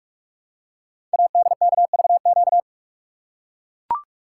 Morse code, heard as a keyed tone, about 44 wpm, 700 Hz. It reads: ADKVQ
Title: Code Groups - 5 Letters - 45wpm